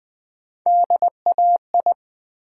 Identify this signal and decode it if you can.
Morse code, heard as a keyed tone, about 20 wpm, 700 Hz. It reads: DAI
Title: Build A Word - 1 Letter At A Time - 5-Letter Words - 20wpm